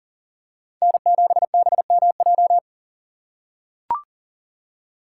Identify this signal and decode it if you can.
Morse code, heard as a keyed tone, about 40 wpm, 700 Hz. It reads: N7BMJ
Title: USA Call Signs - 5 Characters - 4x Character Spacing - 40wpm